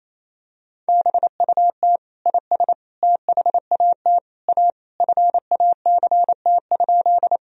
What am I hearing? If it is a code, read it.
Morse code, heard as a keyed tone, about 28 wpm, 700 Hz. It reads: BUT IS THAT A FACT?
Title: Sentences from Top 1000 Words - Review Part 3 - 28wpm